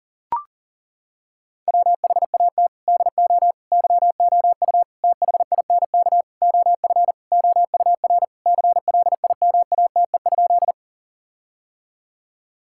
Morse code, heard as a keyed tone, about 40 wpm, 700 Hz. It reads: WHAT DO YOU THINK OF OUR CLIMATE?
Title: Sentences from Top 1600 Words 40wpm